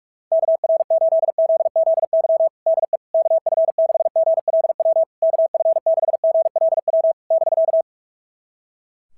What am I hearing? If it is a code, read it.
Morse code, heard as a keyed tone, about 44 wpm, 660 Hz. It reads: KR8ZZY DE KF6GLW KF6GLW <BK>
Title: SOTA QSO Practice 45wpm